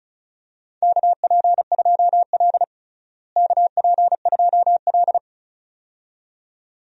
Morse code, heard as a keyed tone, about 36 wpm, 700 Hz. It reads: KP2L KP2L
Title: DXCC-100 Call Signs - 4 Characters - Repeated 2x - 35wpm